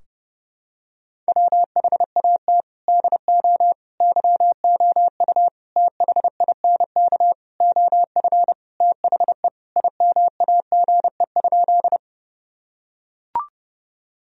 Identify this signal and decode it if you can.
Morse code, heard as a keyed tone, about 30 wpm, 700 Hz. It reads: WHAT DO YOU THINK OF THE IMAGE?
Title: Sentences from Top 1000 Words - Review Part 4 - 30wpm